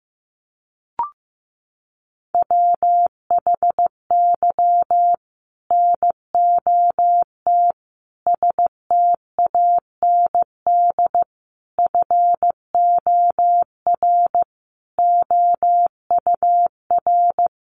Morse code, heard as a keyed tone, about 15 wpm, 700 Hz. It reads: WHY NOT STAND FOR OUR
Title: Sentences from Top 1500 Words 15wpm